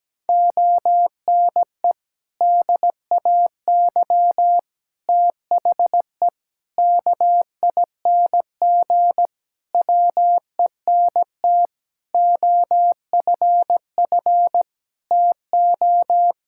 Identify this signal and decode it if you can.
Morse code, heard as a keyed tone, about 17 wpm, 700 Hz. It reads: ONE DAY THE KING WENT OFF TO
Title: Sentences from Top 1600 Words 17wpm